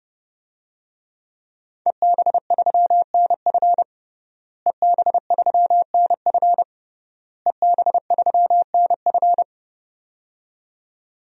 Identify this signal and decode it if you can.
Morse code, heard as a keyed tone, about 30 wpm, 700 Hz. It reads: EB3NF EB3NF EB3NF
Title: DXCC-100 Call Signs - 5 Characters - Repeated 3x - 30wpm